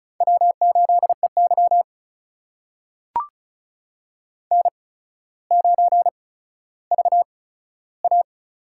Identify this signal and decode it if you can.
Morse code, heard as a keyed tone, about 36 wpm, spaced as for 15 wpm, 700 Hz. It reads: W8EY N9VA
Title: USA Call Signs - 8x Character Spacing - 35wpm